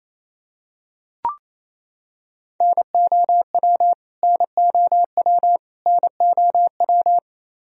Morse code, heard as a keyed tone, about 28 wpm, 700 Hz. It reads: NOW NOW NOW
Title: Top 100 Words Repeated 28wpm